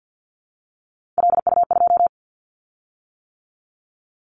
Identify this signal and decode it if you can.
Morse code, heard as a keyed tone, about 50 wpm, 690 Hz. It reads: LU2
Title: Callsign Prefixes for Top 100 DXCC - 50wpm